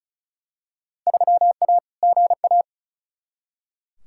Morse code, heard as a keyed tone, about 36 wpm, 700 Hz. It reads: 3A GA
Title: Field Day Exchanges - 35wpm